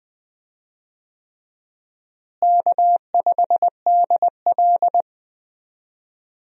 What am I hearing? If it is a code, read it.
Morse code, heard as a keyed tone, about 20 wpm, 700 Hz. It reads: K5DL
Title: USA Call Signs - 4 Characters - Repeated 3x - 20wpm